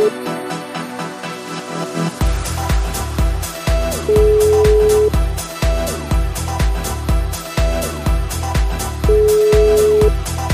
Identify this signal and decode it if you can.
Ringback tone (Europe, 425 Hz), with noise.